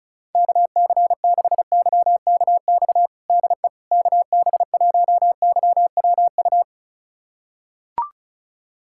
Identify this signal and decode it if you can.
Morse code, heard as a keyed tone, about 34 wpm, 700 Hz. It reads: KC6YKX DE KB1YWU
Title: USA Callsign DE Callsign - 6 Characters - 35wpm